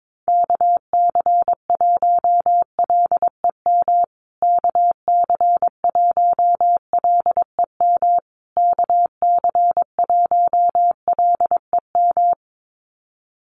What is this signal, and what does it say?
Morse code, heard as a keyed tone, about 22 wpm, 700 Hz. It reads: KC1LEM KC1LEM KC1LEM
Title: USA Call Signs Repeated 3x 22wpm